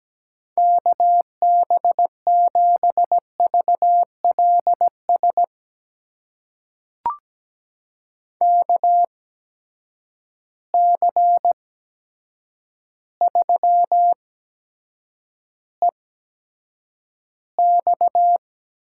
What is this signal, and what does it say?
Morse code, heard as a keyed tone, about 17 wpm, spaced as for 8 wpm, 700 Hz. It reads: KB7VLS KC3EX
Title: USA Call Signs - 6 Characters - 8x Character Spacing - 17wpm